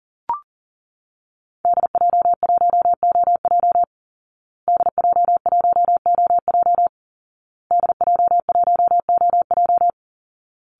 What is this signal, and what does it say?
Morse code, heard as a keyed tone, about 40 wpm, 700 Hz. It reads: DJ1OJ DJ1OJ DJ1OJ
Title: Call Signs Repeated 2x - 40wpm